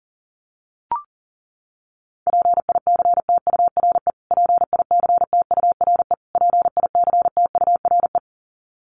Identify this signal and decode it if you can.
Morse code, heard as a keyed tone, about 40 wpm, 700 Hz. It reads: PICTURE PICTURE PICTURE
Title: Top 500 Words Repeated 40wpm